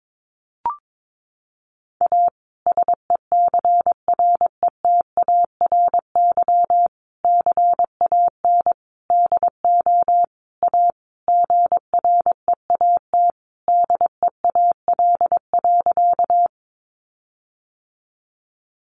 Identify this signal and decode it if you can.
Morse code, heard as a keyed tone, about 22 wpm, 700 Hz. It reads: A SECRETARY CAN DO A GREAT DEAL.
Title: Sentences from Top 1300 Words 22wpm